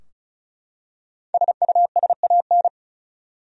Morse code, heard as a keyed tone, about 36 wpm, 700 Hz. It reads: SUSAN
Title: USA Common Names - 35wpm